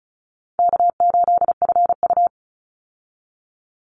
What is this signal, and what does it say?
Morse code, heard as a keyed tone, about 34 wpm, 700 Hz. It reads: K8FU
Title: Easy Call Signs Repeated 3x - 35wpm